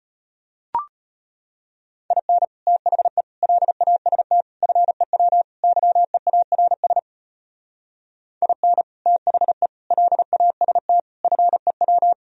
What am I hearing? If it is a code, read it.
Morse code, heard as a keyed tone, about 35 wpm, 700 Hz. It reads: IN THE LAST FEW YEARS IN THE LAST FEW